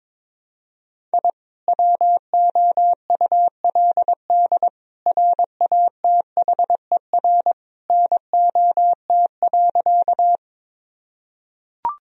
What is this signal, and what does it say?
Morse code, heard as a keyed tone, about 22 wpm, 700 Hz. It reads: I WOULD RATHER NOT.
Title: Sentences from Top 1000 Words - Review Part 4 - 22wpm